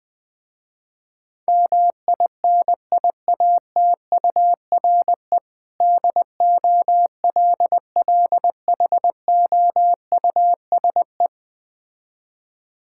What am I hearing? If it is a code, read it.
Morse code, heard as a keyed tone, about 20 wpm, 700 Hz. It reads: MINIATURE DOLLHOUSE
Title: Difficult Words with Hints - 20wpm